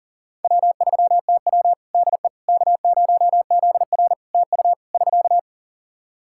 Morse code, heard as a keyed tone, about 40 wpm, 700 Hz. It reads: W3TW DE K0ZR TU <SK>
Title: K1USN SST QSO Practice 40wpm